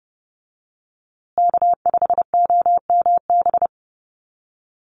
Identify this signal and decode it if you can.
Morse code, heard as a keyed tone, about 30 wpm, 700 Hz. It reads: K5OMB
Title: USA Call Signs Repeated 2x 30wpm